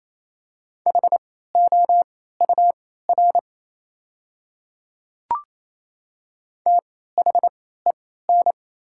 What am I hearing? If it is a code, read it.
Morse code, heard as a keyed tone, about 28 wpm, spaced as for 16 wpm, 690 Hz. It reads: HOUR THEN
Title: 3x Character Spacing - Top 500 Words - 28wpm